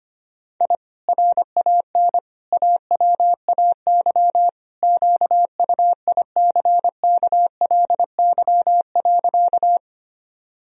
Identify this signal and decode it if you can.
Morse code, heard as a keyed tone, about 25 wpm, 700 Hz. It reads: I RAN AWAY QUICKLY.
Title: Sentences from Top 1000 Words - Review Part 3 - 25wpm